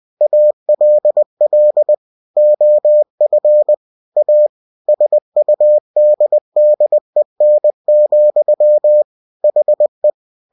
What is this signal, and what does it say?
Morse code, heard as a keyed tone, about 20 wpm, 600 Hz. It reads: ALL OF A SUDDEN, HE
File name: cw-books / winnie-the-pooh / winnie-20wpm-page-0019